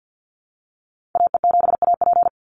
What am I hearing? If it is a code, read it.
Morse code, heard as a keyed tone, about 50 wpm, 700 Hz. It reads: AE7AP
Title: NA SOTA Activator Callsigns 50wpm